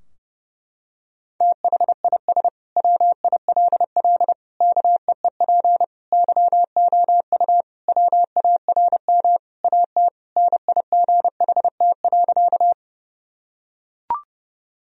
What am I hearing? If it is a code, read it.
Morse code, heard as a keyed tone, about 30 wpm, 700 Hz. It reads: THIS WILL KEEP YOU WARM AT NIGHT.
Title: Sentences from Top 1000 Words - Review Part 2 - 30wpm